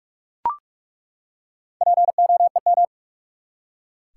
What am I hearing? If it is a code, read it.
Morse code, heard as a keyed tone, about 42 wpm, 700 Hz. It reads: POEM
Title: Four Letter Words - 45wpm